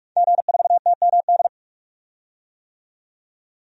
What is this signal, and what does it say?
Morse code, heard as a keyed tone, about 48 wpm, 700 Hz. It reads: G4TMD
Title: DXCC-100 Call Signs - 5 Characters - 45wpm